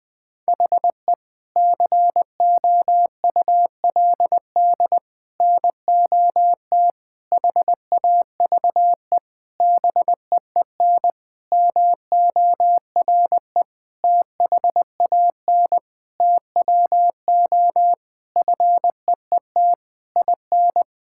Morse code, heard as a keyed tone, about 21 wpm, 700 Hz. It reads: HE COULD NOT HAVE BEEN MORE THAN TWO FEET IN
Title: Sentences from Top 1600 Words 20wpm